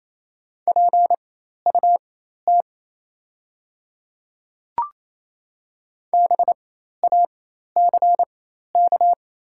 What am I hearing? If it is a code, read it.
Morse code, heard as a keyed tone, about 28 wpm, spaced as for 13 wpm, 690 Hz. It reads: PUT BACK
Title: 4x Character Spacing - Top 500 Words - 28wpm